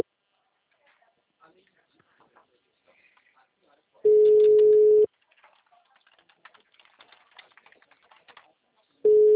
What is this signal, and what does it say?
Ringback tone (Europe, 425 Hz), with speech.